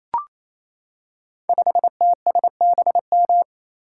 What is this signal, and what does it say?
Morse code, heard as a keyed tone, about 27 wpm, 700 Hz. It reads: 5TSBM
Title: Code Groups - 5 Letter:Numbers - 28wpm